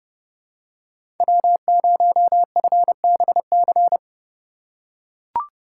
Morse code, heard as a keyed tone, about 30 wpm, 700 Hz. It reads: W0FBC
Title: USA Call Signs - 2x Character Spacing - 30wpm